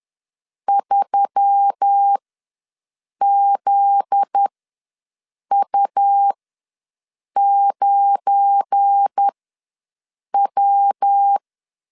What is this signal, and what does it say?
Morse code, heard as a keyed tone, about 10 wpm, spaced as for 6 wpm, 800 Hz. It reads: 3ZU9W
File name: LEZIONI / 7 GF9Z / 7 GF9Z ws 10wpm 3919